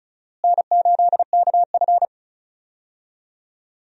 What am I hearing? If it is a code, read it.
Morse code, heard as a keyed tone, about 34 wpm, 700 Hz. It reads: N8KF
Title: USA Call Signs - 4 Characters - Repeated 2x - 35wpm